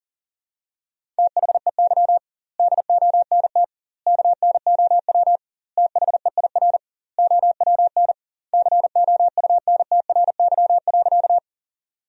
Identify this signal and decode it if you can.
Morse code, heard as a keyed tone, about 40 wpm, 700 Hz. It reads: THEY DONT KNOW THEIR OWN COUNTRY.
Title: 2x Word-Spacing - Sentences from Top 300 Words - 40wpm